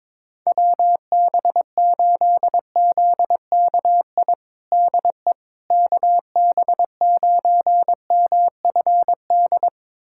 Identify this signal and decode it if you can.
Morse code, heard as a keyed tone, about 22 wpm, 700 Hz. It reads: WB8ZKI DE KB9MFD
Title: USA Callsign DE Callsign - 6 Characters - 22wpm